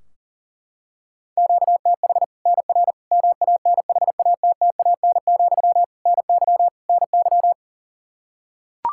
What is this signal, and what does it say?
Morse code, heard as a keyed tone, about 37 wpm, 700 Hz. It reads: QTH NR MANHATTAN, NY NY
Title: Practice QSO - Exchange 1 - Beginning - 40wpm